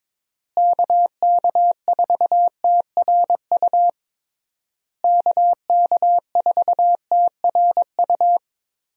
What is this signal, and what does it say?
Morse code, heard as a keyed tone, about 22 wpm, 700 Hz. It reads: KK4TRU KK4TRU
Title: USA Call Signs - 6 Characters - Repeated 3x - 22wpm